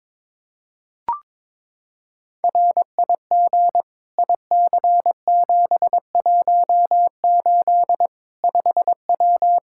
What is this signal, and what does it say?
Morse code, heard as a keyed tone, about 22 wpm, 700 Hz. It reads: RIG IC718 5W
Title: K1USN SST QSO Practice 22wpm